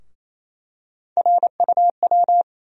Morse code, heard as a keyed tone, about 28 wpm, 700 Hz. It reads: RUW
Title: USA Call Sign Postfixes - 3 Letters 28wpm